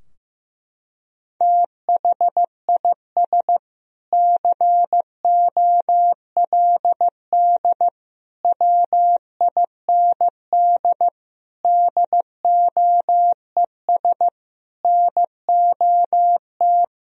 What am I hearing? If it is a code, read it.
Morse code, heard as a keyed tone, about 15 wpm, 700 Hz. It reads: THIS COLD WIND DOES NOT
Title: Sentences from Top 1000 Words - Review Part 2 - 15wpm